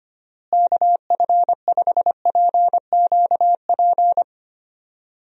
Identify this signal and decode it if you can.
Morse code, heard as a keyed tone, about 25 wpm, 700 Hz. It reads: KF5PQP
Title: USA Call Signs - 6 Characters - Repeated 2x - 25wpm